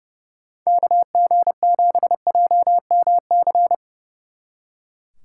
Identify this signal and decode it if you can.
Morse code, heard as a keyed tone, about 29 wpm, 700 Hz. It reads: KG7JMC
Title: Speed Racing - USA 2x3 Call Signs 30wpm